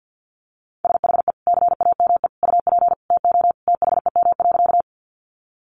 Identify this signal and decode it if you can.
Morse code, heard as a keyed tone, about 48 wpm, 700 Hz. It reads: SHE CAME UP TO THEM.
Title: Sentences from Top 200 Words 50wpm